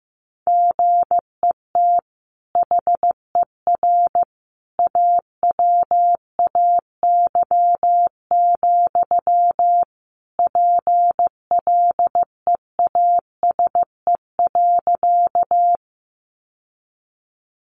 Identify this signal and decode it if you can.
Morse code, heard as a keyed tone, about 15 wpm, 700 Hz. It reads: GET HER AWAY, PLEASE.
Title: Sentences from Top 1500 Words 15wpm